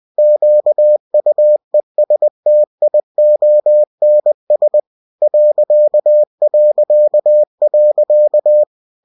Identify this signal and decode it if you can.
Morse code, heard as a keyed tone, about 20 wpm, 600 Hz. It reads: QUESTIONS ...
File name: cw-books / winnie-the-pooh / winnie-20wpm-page-0041